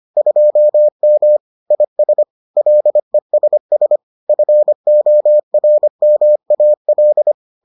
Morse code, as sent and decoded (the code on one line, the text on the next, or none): ..--- -- / .. ... / .-.. ... ... / ..-. --- .-. -- .- .-..
2M IS LESS FORMAL